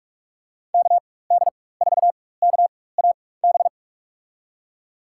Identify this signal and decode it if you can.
Morse code, heard as a keyed tone, about 43 wpm, spaced as for 21 wpm, 700 Hz. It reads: KD4KAB
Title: USA Call Signs - 6 Characters - 4x Character Spacing - 45wpm